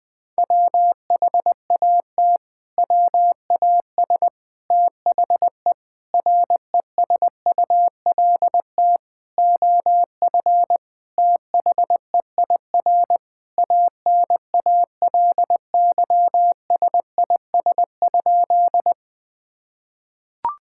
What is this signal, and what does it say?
Morse code, heard as a keyed tone, about 21 wpm, 700 Hz. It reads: WHAT WAS THE RESULT OF THEIR ANALYSIS?
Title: Sentences from Top 1000 Words - Review Part 5 - 20wpm